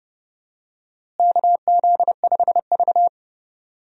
Morse code, heard as a keyed tone, about 32 wpm, 700 Hz. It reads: KZ5V
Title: USA Call Signs - 4 Characters - Repeated 3x - 30wpm